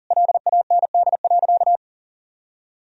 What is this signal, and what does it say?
Morse code, heard as a keyed tone, about 37 wpm, 700 Hz. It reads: LAND.
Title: Sentences from Top 1000 Words - Review Part 2 - 40wpm